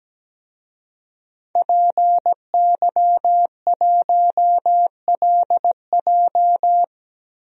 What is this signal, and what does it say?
Morse code, heard as a keyed tone, about 17 wpm, 700 Hz. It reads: PY1LJ
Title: DXCC-100 Call Signs - 5 Characters - Repeated 2x - 17wpm